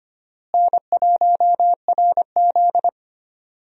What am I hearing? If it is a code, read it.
Morse code, heard as a keyed tone, about 25 wpm, 700 Hz. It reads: N1RZ
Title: USA Call Signs - 4 Characters - Repeated 3x - 25wpm